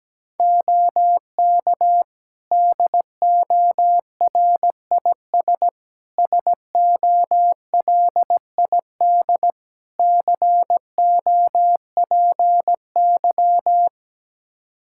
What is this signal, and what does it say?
Morse code, heard as a keyed tone, about 17 wpm, 700 Hz. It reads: OK DORIS SOLID COPY
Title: Practice QSO - Exchange 3 - More Info - 17wpm